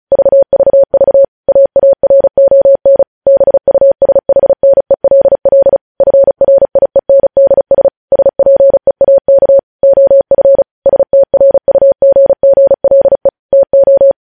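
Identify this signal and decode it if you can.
Morse code, heard as a keyed tone, about 36 wpm, 550 Hz. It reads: VVV AARON BUSHNELL FRIENDS SPEAK OF STRUGGLE TO
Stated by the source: Morse practice at 35 wpm sending news headlines.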